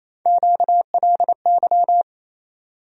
Morse code, heard as a keyed tone, about 27 wpm, 700 Hz. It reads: QLY
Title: USA Call Sign Postfixes - 3 Letters 28wpm